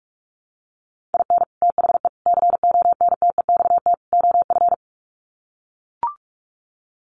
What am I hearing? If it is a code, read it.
Morse code, heard as a keyed tone, about 44 wpm, 700 Hz. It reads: IN THE CONTEXT OF
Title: Sets of 4 Words 45wpm